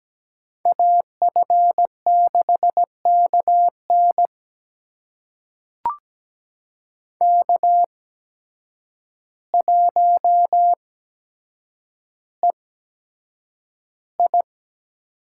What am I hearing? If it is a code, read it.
Morse code, heard as a keyed tone, about 17 wpm, spaced as for 8 wpm, 700 Hz. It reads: AF6KN K1EI
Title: USA Call Signs - 5 Characters - 8x Character Spacing - 17wpm